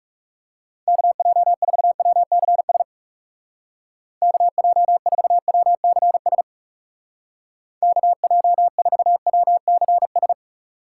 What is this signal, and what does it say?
Morse code, heard as a keyed tone, about 39 wpm, 700 Hz. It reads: KJ4WCS KJ4WCS KJ4WCS